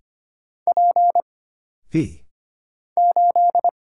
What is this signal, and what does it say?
Morse code, heard as a keyed tone, about 25 wpm, 700 Hz. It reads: P 8
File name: Single Character (Letter:Number:Punctuation:Prosign) - Mind-Melt - 25wpm